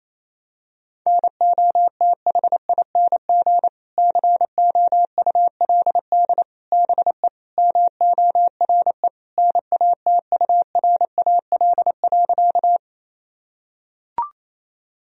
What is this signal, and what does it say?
Morse code, heard as a keyed tone, about 28 wpm, 700 Hz. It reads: NOTHING COULD BE MORE NATURAL.
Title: Sentences from Top 1000 Words - Review Part 4 - 28wpm